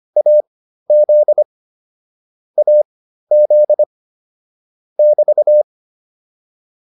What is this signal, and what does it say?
Morse code, heard as a keyed tone, about 25 wpm, spaced as for 13 wpm, 600 Hz. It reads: AZ AZ <BT>